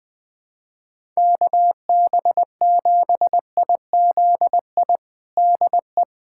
Morse code, heard as a keyed tone, about 20 wpm, 700 Hz. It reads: KB7IZI DE
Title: USA Callsign DE Callsign - 6 Characters - 20wpm